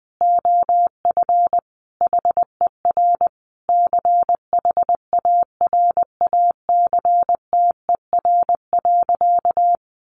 Morse code, heard as a keyed tone, about 20 wpm, 700 Hz. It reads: OF HER CHARACTER.
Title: Sentences from Top 1200 Words 20wpm